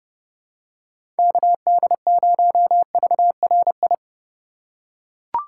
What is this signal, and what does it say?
Morse code, heard as a keyed tone, about 31 wpm, 700 Hz. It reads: KD0VRI
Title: USA Call Signs - 6 Characters - 3x Character Spacing - 30wpm